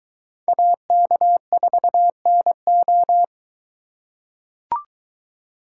Morse code, heard as a keyed tone, about 22 wpm, 700 Hz. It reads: AK4NO